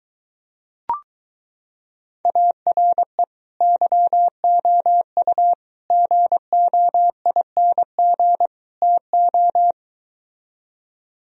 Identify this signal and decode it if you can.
Morse code, heard as a keyed tone, about 23 wpm, 700 Hz. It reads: ARE YOU GOING TO